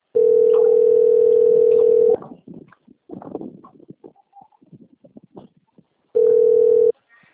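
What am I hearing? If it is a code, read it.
Ringback tone (North America), 440 + 480 Hz.
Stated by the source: Outbound telephone call recorded on the Cornelio Procopio system.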